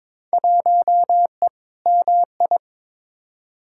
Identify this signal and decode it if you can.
Morse code, heard as a keyed tone, about 22 wpm, 700 Hz. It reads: 1E MI
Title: Field Day Exchanges - 22wpm